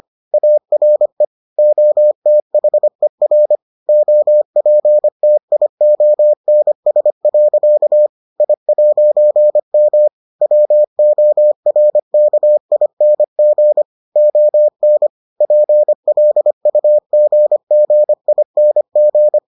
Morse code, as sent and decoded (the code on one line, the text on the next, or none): .- .-. / --- - .... .-. / --- .--. - .. --- -. ... .-.-.- / .. .----. -- / .-- --- .-. -.- .. -. --. / --- -. / .--. .-.. ..- --. --. .. -. --.
ARE OTHER OPTIONS. I'M WORKING ON PLUGGING